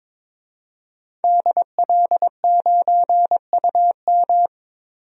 Morse code, heard as a keyed tone, about 22 wpm, 700 Hz. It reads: DL9UM
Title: DXCC-100 Call Signs - 5 Characters - 22wpm